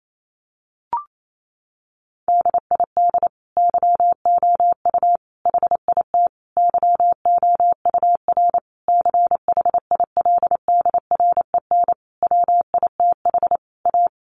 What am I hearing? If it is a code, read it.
Morse code, heard as a keyed tone, about 28 wpm, 700 Hz. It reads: DID YOU HIT YOUR CHILDREN WITH A
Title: Sentences from Top 1400 Words 28wpm